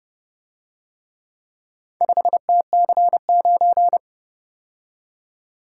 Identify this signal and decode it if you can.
Morse code, heard as a keyed tone, about 29 wpm, 700 Hz. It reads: 5TC9